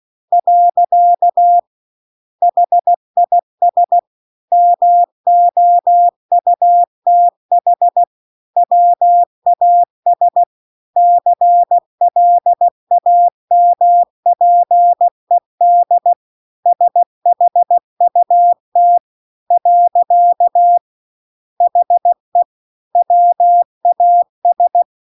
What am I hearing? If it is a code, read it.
Morse code, heard as a keyed tone, about 16 wpm, 700 Hz. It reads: . HIS MOUTH WAS CLAMPED SHUT . HE WAS